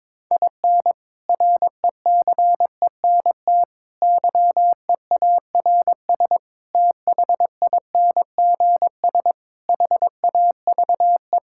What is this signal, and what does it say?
Morse code, heard as a keyed tone, about 22 wpm, 700 Hz. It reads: IN RECENT YEARS THINGS HAVE
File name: practice / 22wpm / Sentences with 3-Word Phrases - 22wpm